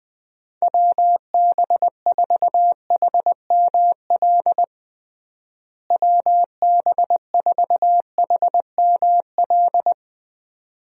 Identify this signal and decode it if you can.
Morse code, heard as a keyed tone, about 20 wpm, 700 Hz. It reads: WB4HML WB4HML
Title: USA Call Signs - 6 Characters - Repeated 2x - 20wpm